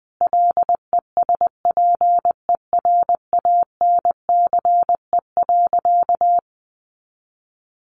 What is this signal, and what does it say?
Morse code, heard as a keyed tone, about 20 wpm, 700 Hz. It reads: LESPERANCE.
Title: A Sound Of Thunder 20wpm